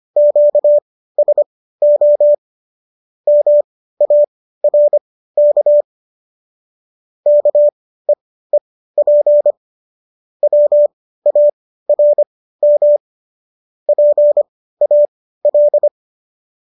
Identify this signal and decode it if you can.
Morse code, heard as a keyed tone, about 25 wpm, spaced as for 15 wpm, 590 Hz. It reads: QSO MARK KEEP WARM PAL